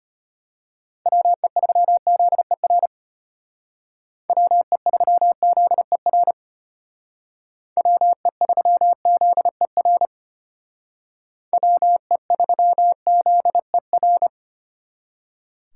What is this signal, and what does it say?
Morse code, heard as a keyed tone, about 33 wpm, 700 Hz. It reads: WE3ZER WE3ZER WE3ZER WE3ZER